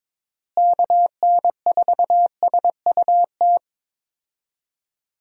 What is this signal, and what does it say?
Morse code, heard as a keyed tone, about 22 wpm, 700 Hz. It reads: KN4SUT
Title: USA Call Signs - 6 Characters - Repeated 2x - 22wpm